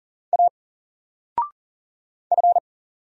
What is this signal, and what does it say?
Morse code, heard as a keyed tone, about 38 wpm, 700 Hz. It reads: A F
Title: Single Letter-Number 40wpm